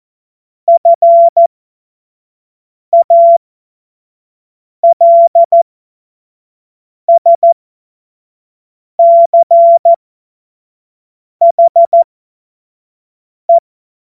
Morse code, heard as a keyed tone, about 14 wpm, spaced as for 5 wpm, 680 Hz. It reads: FALSCHE
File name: Morsekurs / Witze / Witz 16. November 2021, 25 F70